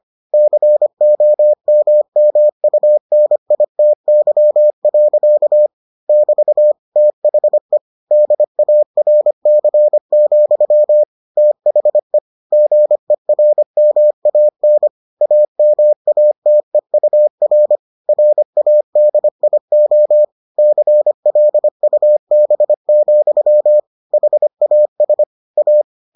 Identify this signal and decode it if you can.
Morse code, heard as a keyed tone, about 25 wpm, 600 Hz. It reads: COMMUNITY. <BT> THE DARC, THE GERMAN AMATEUR RADIO CLUB, HAS A